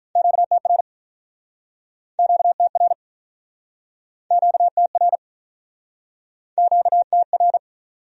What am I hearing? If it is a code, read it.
Morse code, heard as a keyed tone, about 48 wpm, 700 Hz. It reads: QTR QTR QTR QTR